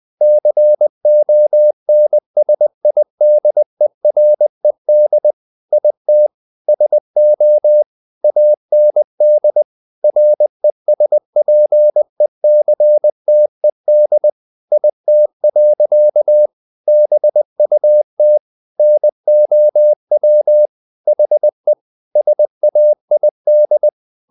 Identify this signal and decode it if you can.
Morse code, heard as a keyed tone, about 20 wpm, 600 Hz. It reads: CONSIDERED IT SO AND RESPECTED IT. BUT NOW HE SAID
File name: the old man and the sea / the old man and the sea part 1 20wpm